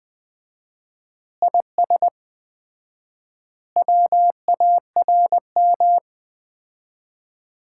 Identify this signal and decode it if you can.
Morse code, heard as a keyed tone, about 20 wpm, 700 Hz. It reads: IS WARM
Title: Common QSO Phrases - 4x Word-Spacing - 20wpm